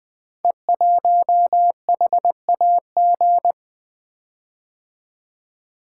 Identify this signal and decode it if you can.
Morse code, heard as a keyed tone, about 20 wpm, 700 Hz. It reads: E1HAG
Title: Code Groups - 5 Letter:Numbers - 20wpm